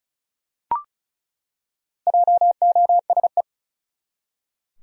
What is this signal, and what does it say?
Morse code, heard as a keyed tone, about 36 wpm, 700 Hz. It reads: JOSE
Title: USA Common Names - Max 4 Letters - 35wpm